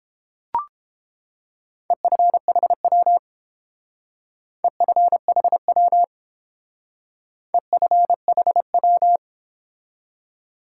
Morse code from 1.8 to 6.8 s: EFHW EFHW

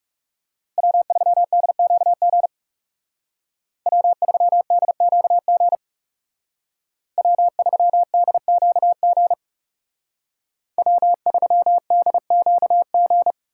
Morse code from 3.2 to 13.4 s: W3DQG W3DQG W3DQG

